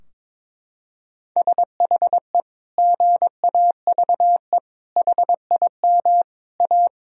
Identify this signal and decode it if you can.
Morse code, heard as a keyed tone, about 22 wpm, 700 Hz. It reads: SHE GAVE HIM A
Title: Sentences from Top 1000 Words - Review Part 2 - 22wpm